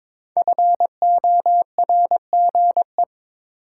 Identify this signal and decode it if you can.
Morse code, heard as a keyed tone, about 22 wpm, 700 Hz. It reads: FORGE
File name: 5 Letter Words - First 3 Letters Given - 22wpm